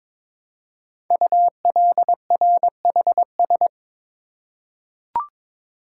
Morse code, heard as a keyed tone, about 22 wpm, 700 Hz. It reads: ULRHS
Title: Code Groups - 5 Letters - 22wpm